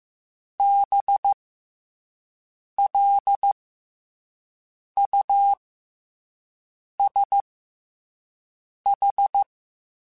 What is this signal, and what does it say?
Morse code, heard as a keyed tone, about 15 wpm, 800 Hz. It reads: B L U S H
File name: Special INGs-mp3s / Special INGs-mp3s / SlowToFaster ing @10